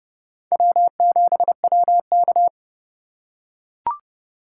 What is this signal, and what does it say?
Morse code, heard as a keyed tone, about 29 wpm, 700 Hz. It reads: W7WK